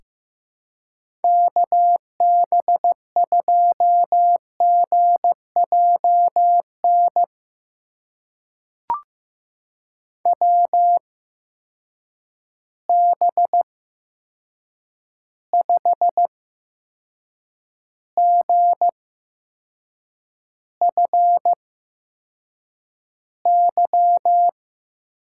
Morse code, heard as a keyed tone, about 15 wpm, spaced as for 6 wpm, 700 Hz. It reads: KB2GJN WB5GFY
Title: USA Call Signs - 6 Characters - 8x Character Spacing - 15wpm